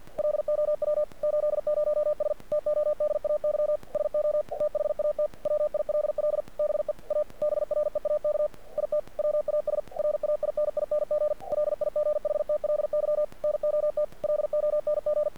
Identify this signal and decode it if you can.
Morse code, heard as a keyed tone, about 48 wpm, 600 Hz. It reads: LOW 80S TODAY SO THAT WILL BE A BREAK IT WAS RAINING LIGHTLY NOT LONG
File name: archived / CRSnet